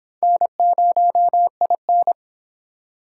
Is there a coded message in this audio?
Yes: Morse code from 0.2 to 2.1 s, heard as a keyed tone, about 23 wpm, 700 Hz. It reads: N0IN